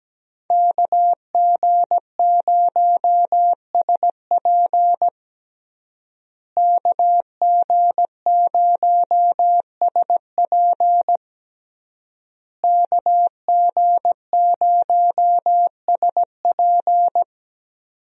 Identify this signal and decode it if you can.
Morse code, heard as a keyed tone, about 17 wpm, 700 Hz. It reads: KG0SP KG0SP KG0SP
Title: USA Call Signs - 5 Characters - Repeated 3x - 17wpm